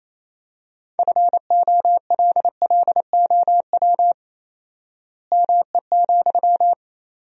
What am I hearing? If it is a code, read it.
Morse code, heard as a keyed tone, about 27 wpm, 700 Hz. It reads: FOLLOW ME,
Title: 4x Word-Spacing - Sentences from Top 200 Words - 28wpm